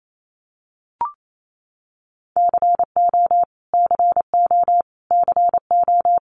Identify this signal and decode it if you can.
Morse code, heard as a keyed tone, about 28 wpm, 700 Hz. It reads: CO CO CO
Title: US States Repeated 3x 28wpm